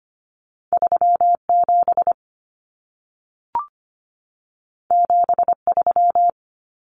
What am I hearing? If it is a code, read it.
Morse code, heard as a keyed tone, about 25 wpm, 700 Hz. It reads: 37 73
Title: Common Cut Numbers - 2-Digit 25wpm